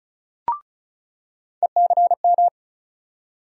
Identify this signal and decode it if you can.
Morse code, heard as a keyed tone, about 36 wpm, 700 Hz. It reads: ECM
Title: USA Call Sign Postfixes - 3 Letters 35wpm